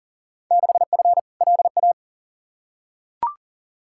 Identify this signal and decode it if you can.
Morse code, heard as a keyed tone, about 37 wpm, 700 Hz. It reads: 6F LA
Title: Field Day Exchanges Encore - 40wpm